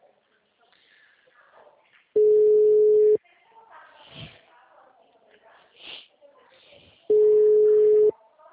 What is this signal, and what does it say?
Ringback tone (Europe, 425 Hz), with speech.